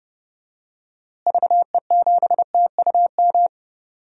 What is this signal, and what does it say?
Morse code, heard as a keyed tone, about 30 wpm, 700 Hz. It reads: VE7TUM
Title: DXCC-100 Call Signs - 6 Characters - Repeated 2x - 30wpm